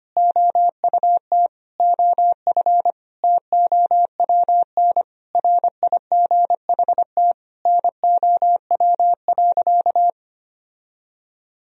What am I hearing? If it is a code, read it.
Morse code, heard as a keyed tone, about 25 wpm, 700 Hz. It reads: OUT OF TOWN RIGHT NOW.
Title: Sentences from Top 1000 Words - Review Part 4 - 25wpm